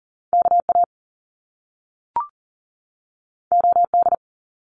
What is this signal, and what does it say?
Morse code, heard as a keyed tone, about 42 wpm, 700 Hz. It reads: KA OD
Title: Top Two Letter Combinations in Words 40wpm